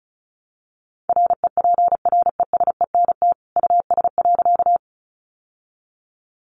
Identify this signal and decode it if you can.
Morse code, heard as a keyed tone, about 34 wpm, 700 Hz. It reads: REPRESENT US.
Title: Sentences from Top 1200 Words 35wpm